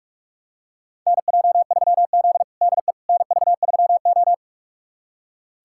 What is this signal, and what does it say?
Morse code, heard as a keyed tone, about 44 wpm, 700 Hz. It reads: NJ3Z DE NV3O
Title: USA Callsign DE Callsign - 4 Characters - 45wpm